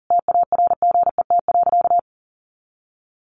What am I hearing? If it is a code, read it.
Morse code, heard as a keyed tone, about 37 wpm, 700 Hz. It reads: TARGET.
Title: Sentences from Top 1200 Words 40wpm